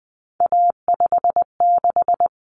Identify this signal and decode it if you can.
Morse code, heard as a keyed tone, about 20 wpm, 700 Hz. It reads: A56
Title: Common Cut Numbers - 3-Digit 20wpm